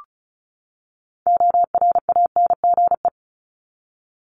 Morse code, heard as a keyed tone, about 34 wpm, 700 Hz. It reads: ORANGE
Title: Top 1751-2000 Words 35wpm